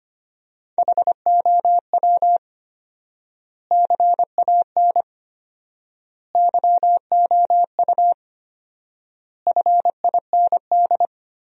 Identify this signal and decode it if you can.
Morse code, heard as a keyed tone, about 25 wpm, 700 Hz. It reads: HOW CAN YOU FIND
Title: 4x Word-Spacing - Sentences from Top 200 Words - 25wpm